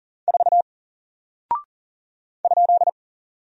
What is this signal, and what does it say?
Morse code, heard as a keyed tone, about 38 wpm, 700 Hz. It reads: 4 ?
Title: Single Character (Letter:Number:Punctuation:Prosign) 40wpm